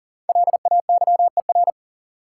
Morse code, heard as a keyed tone, about 40 wpm, 700 Hz. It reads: LAYER
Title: Sentences from Top 1600 Words - New Word Drill - 40wpm